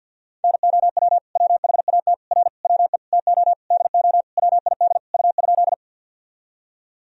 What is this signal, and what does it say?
Morse code, heard as a keyed tone, about 52 wpm, 700 Hz. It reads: NOW WHAT R WE TO DO WID U?
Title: Sentences from Top 100 Words and Abbreviations 50wpm